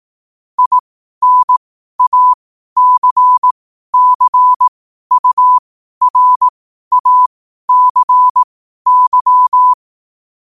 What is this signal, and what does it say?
Morse code, heard as a keyed tone, about 18 wpm, spaced as for 12 wpm, 1000 Hz. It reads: INACCURACY
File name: qotdfiles / 12wpm